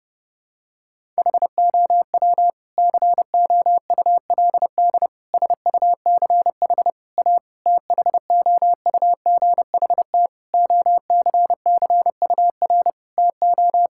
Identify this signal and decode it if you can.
Morse code, heard as a keyed tone, about 30 wpm, 700 Hz. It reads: HOW COULD SUCH A THOUGHT OCCUR TO
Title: Sentences from Top 1000 Words - Review Part 6 - 30wpm